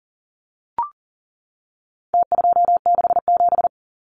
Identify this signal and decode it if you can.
Morse code, heard as a keyed tone, about 40 wpm, 700 Hz. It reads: T267
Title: Numbers - Contest Style - 4-Digit 40wpm